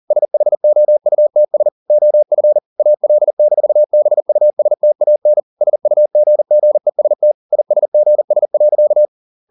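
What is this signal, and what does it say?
Morse code, heard as a keyed tone, about 40 wpm, 600 Hz. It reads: SHOUTS OF AL-BUSTAN SUGGEST ISGS.
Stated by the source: Morse code at 40 wpm, practice text, CW med Gnister.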